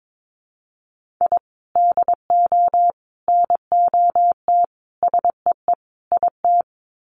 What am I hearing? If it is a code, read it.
Morse code, heard as a keyed tone, about 22 wpm, 700 Hz. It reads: I DO NOT SEE IT
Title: Sets of 5 Words Encore2 22wpm